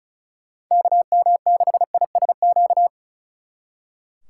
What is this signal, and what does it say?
Morse code, heard as a keyed tone, about 45 wpm, 700 Hz. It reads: KM6ISQ